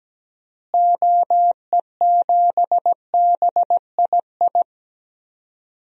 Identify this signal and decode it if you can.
Morse code, heard as a keyed tone, about 17 wpm, 700 Hz. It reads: OE7BII
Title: DXCC-100 Call Signs - 6 Characters - Repeated 3x - 17wpm